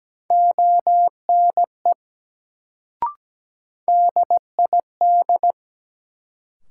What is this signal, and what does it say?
Morse code, heard as a keyed tone, about 17 wpm, 700 Hz. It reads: ONE DID
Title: Sentences from Top 100 Words - New Word Drill - 17wpm